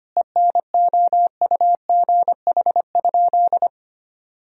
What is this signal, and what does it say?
Morse code, heard as a keyed tone, about 25 wpm, 700 Hz. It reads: ENOUGH?
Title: Sentences from Top 1000 Words - Review Part 3 - 25wpm